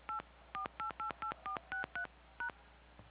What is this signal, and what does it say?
DTMF, heard as telephone touch tones, tones of about 110 ms. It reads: 84888463#